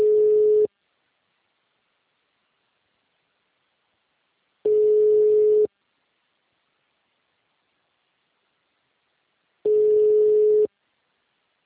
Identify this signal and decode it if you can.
Ringback tone (Europe, 425 Hz).
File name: out- -2028